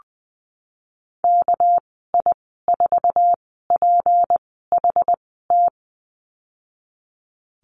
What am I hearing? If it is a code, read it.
Morse code, heard as a keyed tone, about 19 wpm, 700 Hz. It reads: K I 4 P H T
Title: 238 Lesson 38g - Letter q - Call Signs 20wpm